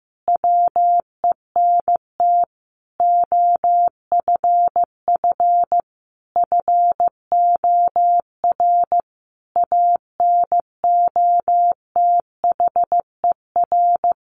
Morse code, heard as a keyed tone, about 15 wpm, 700 Hz. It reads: WENT OFF FOR ANOTHER
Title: Sentences from Top 1500 Words 15wpm